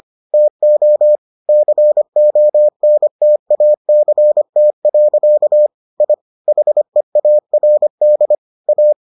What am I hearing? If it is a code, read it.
Morse code, heard as a keyed tone, about 25 wpm, 600 Hz. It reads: TO CONTACT. I HEARD A